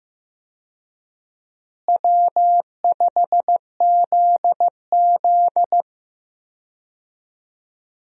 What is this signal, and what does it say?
Morse code, heard as a keyed tone, about 15 wpm, 700 Hz. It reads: W5ZZ
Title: USA Call Signs - 4 Characters - Repeated 3x - 15wpm